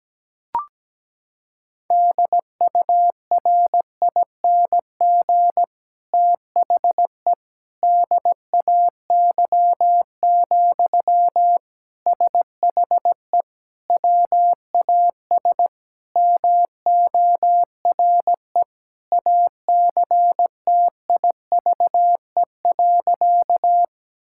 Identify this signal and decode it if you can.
Morse code, heard as a keyed tone, about 17 wpm, 700 Hz. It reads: DURING THE DAY, SHE WAS MORE ACTIVE.
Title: Sentences from Top 1000 Words - Review Part 6 - 17wpm